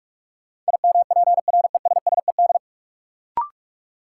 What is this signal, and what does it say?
Morse code, heard as a keyed tone, about 42 wpm, 700 Hz. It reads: IMPRESSED
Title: Sentences from Top 1600 Words - New Word Drill - 45wpm